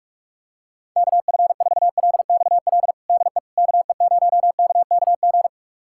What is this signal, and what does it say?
Morse code, heard as a keyed tone, about 46 wpm, 700 Hz. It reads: KF4LXL DE KE0KKG
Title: USA Callsign DE Callsign - 6 Characters - 45wpm